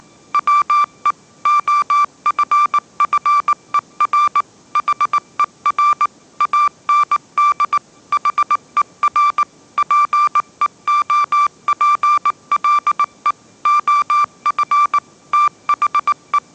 Morse code, heard as a keyed tone, about 19 wpm, 1180 Hz. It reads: WE OFFER HER AND HER PEOPLE OF THE
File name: mrx083 20wpm